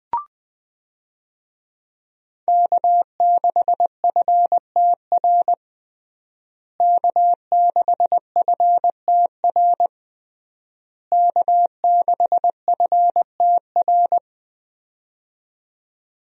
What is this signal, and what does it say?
Morse code, heard as a keyed tone, about 20 wpm, 700 Hz. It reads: K6FTR K6FTR K6FTR
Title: USA Call Signs - 5 Characters - Repeated 3x - 20wpm